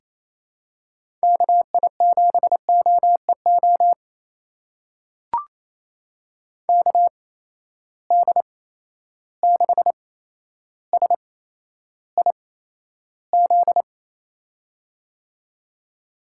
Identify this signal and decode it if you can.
Morse code, heard as a keyed tone, about 28 wpm, spaced as for 12 wpm, 700 Hz. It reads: KI7OEO KD6SIZ
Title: USA Call Signs - 6 Characters - 8x Character Spacing - 28wpm